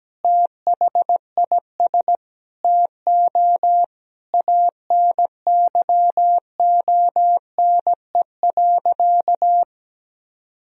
Morse code, heard as a keyed tone, about 17 wpm, 700 Hz. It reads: THIS TO ANYONE.
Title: Sentences from Top 1000 Words - Review Part 2 - 17wpm